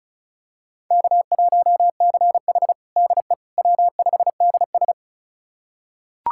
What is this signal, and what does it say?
Morse code, heard as a keyed tone, about 36 wpm, 700 Hz. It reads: K1CH DE W5DS
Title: USA Callsign DE Callsign - 4 Characters - 35wpm